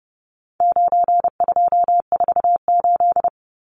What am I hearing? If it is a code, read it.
Morse code, heard as a keyed tone, about 30 wpm, 700 Hz. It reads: 9248
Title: Numbers - 4-Digit 30wpm